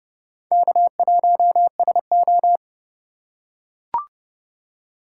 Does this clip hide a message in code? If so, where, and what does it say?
Yes: Morse code from 0.5 to 2.6 s, heard as a keyed tone, about 29 wpm, 700 Hz. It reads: K1SO